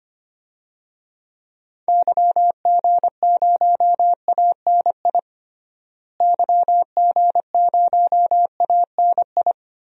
Morse code, heard as a keyed tone, about 25 wpm, 700 Hz. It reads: YG0ANI YG0ANI
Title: DXCC-100 Call Signs - 6 Characters - Repeated 3x - 25wpm